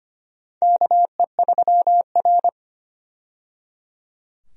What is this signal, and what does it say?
Morse code, heard as a keyed tone, about 25 wpm, 700 Hz. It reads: KE3R
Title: USA Call Signs - 4 Characters - 25wpm